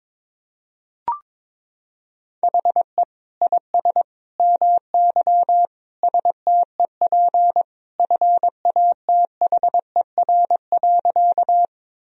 Morse code, heard as a keyed tone, about 22 wpm, 700 Hz. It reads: HE IS MY STEP FATHER.
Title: Sentences from Top 1000 Words - Review Part 2 - 22wpm